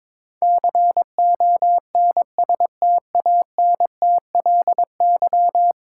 Morse code, heard as a keyed tone, about 21 wpm, 700 Hz. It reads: CONSTANTLY
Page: morse-code-ninja-practice-22wpm